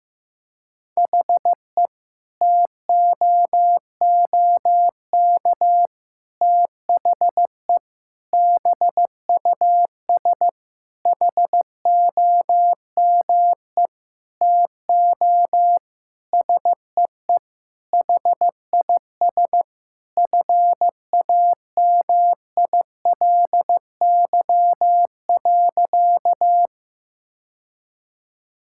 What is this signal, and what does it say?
Morse code, heard as a keyed tone, about 15 wpm, 700 Hz. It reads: HE TOOK THE BUS HOME TO SEE HIS FAMILY.
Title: Sentences from Top 1000 Words - Review Part 6 - 15wpm